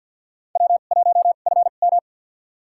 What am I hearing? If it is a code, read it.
Morse code, heard as a keyed tone, about 50 wpm, spaced as for 36 wpm, 700 Hz. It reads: W1FM
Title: USA Call Signs - 2x Character Spacing - 50wpm